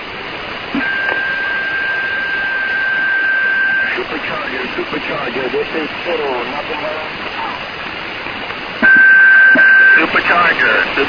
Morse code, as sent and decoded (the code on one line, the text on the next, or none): none